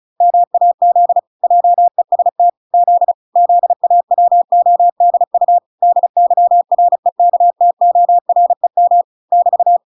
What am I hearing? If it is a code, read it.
Morse code, heard as a keyed tone, about 34 wpm, 700 Hz. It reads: MAZ JEST Z ZAWODU DYREKTOREM <BT>